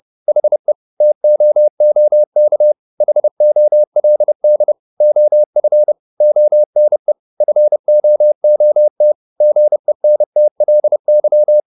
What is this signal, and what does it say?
Morse code, heard as a keyed tone, about 30 wpm, 600 Hz. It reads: HE TOOK HOLD OF ONE FOOT GENTLY
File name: the old man and the sea / the old man and the sea 1 30wpm